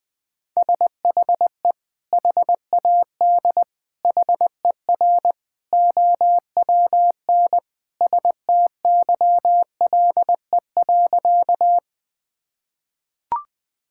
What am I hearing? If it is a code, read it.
Morse code, heard as a keyed tone, about 20 wpm, 700 Hz. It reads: SHE HAD HER OWN STYLE.
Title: Sentences from Top 1000 Words - Review Part 5 - 20wpm